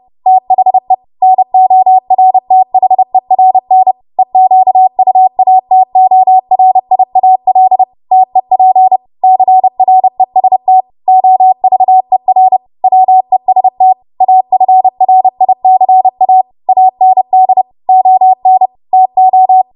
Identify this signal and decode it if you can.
Morse code, heard as a keyed tone, about 30 wpm, 750 Hz. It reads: THE NORTHERN EQUATORIAL TEP CREST OVER WEST AFRICA AND ON TO
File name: Morse / 30WPM